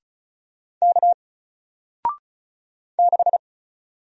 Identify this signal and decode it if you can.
Morse code, heard as a keyed tone, about 36 wpm, 700 Hz. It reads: K 6
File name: Single-Letter-Number-35wpm